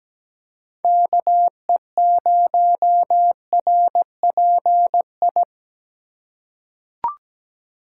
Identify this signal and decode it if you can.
Morse code, heard as a keyed tone, about 17 wpm, 700 Hz. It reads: KE0RPI
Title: USA Call Signs - 6 Characters - 3x Character Spacing - 17wpm